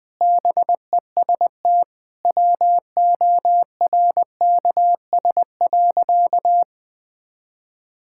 Morse code, heard as a keyed tone, about 21 wpm, 700 Hz. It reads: BEST WORKS.
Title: Sentences with 3-Word Phrases - 20wpm